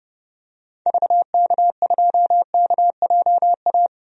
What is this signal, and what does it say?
Morse code, heard as a keyed tone, about 31 wpm, 700 Hz. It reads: VK2KJA
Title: DXCC-100 Call Signs - 6 Characters - 30wpm